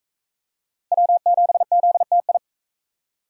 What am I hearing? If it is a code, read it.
Morse code, heard as a keyed tone, about 40 wpm, 700 Hz. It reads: W7ZTI